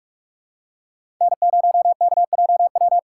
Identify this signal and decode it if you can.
Morse code, heard as a keyed tone, about 41 wpm, 700 Hz. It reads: N0KJW
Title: USA Call Signs - 5 Characters - 2x Character Spacing - 45wpm